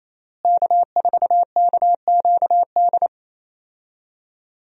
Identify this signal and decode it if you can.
Morse code, heard as a keyed tone, about 27 wpm, 700 Hz. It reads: K4KQD
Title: USA Call Signs - 5 Characters - Repeated 3x - 28wpm